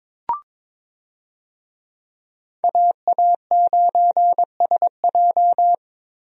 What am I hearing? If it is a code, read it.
Morse code, heard as a keyed tone, about 22 wpm, 700 Hz. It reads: AA9SJ
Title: USA Call Signs - 5 Characters - Repeated 3x - 22wpm